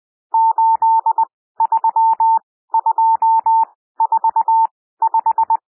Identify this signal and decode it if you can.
Morse code, heard as a keyed tone, about 20 wpm, spaced as for 14 wpm, 900 Hz. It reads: 83245
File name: cwmp3 / N1 SOUND NUMBERS 20 wpm